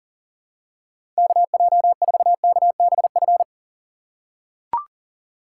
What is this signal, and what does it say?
Morse code, heard as a keyed tone, about 40 wpm, 700 Hz. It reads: KJ4KBF
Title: USA Call Signs - 6 Characters - 6x Character Spacing - 40wpm